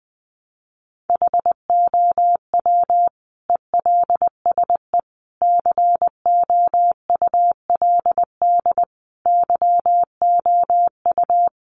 Morse code, heard as a keyed tone, about 20 wpm, 700 Hz. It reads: HOW ELSE COULD YOU
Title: Sentences from Top 1300 Words 20wpm